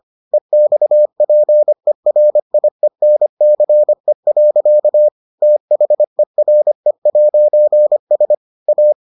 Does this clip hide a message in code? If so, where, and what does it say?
Yes: Morse code from 0.3 to 8.9 s, heard as a keyed tone, about 25 wpm, 600 Hz. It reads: EXPERIENCE. THERE'S A